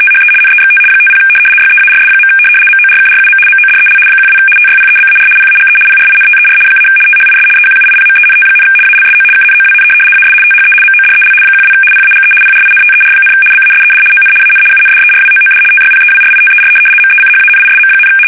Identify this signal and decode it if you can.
RTTY, about 75 baud. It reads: THE NEXT 50 YEARS. MORKRUM MODEL 12 TYPEBAR PAGE PRINTER - THIS EQUIPMENT, ALSO KNOWN AS THE MODEL 12 PAGE PRINTER, BASED ON AN UNDERWOOD TYPEWRITER MECHANISM,
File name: fsk850-75B / teletype-corp